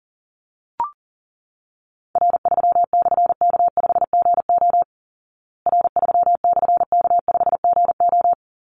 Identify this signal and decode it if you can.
Morse code, heard as a keyed tone, about 40 wpm, 700 Hz. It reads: R3/K5GO R3/K5GO
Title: Call Signs Repeated 3x - 40wpm